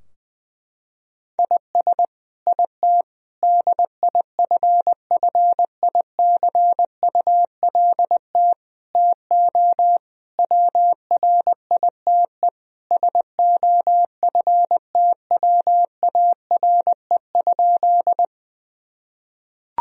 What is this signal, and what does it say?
Morse code, heard as a keyed tone, about 20 wpm, 700 Hz. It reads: IS IT DIFFICULT TO WRITE SOFTWARE?
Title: Sentences from Top 1000 Words - Review Part 3 - 20wpm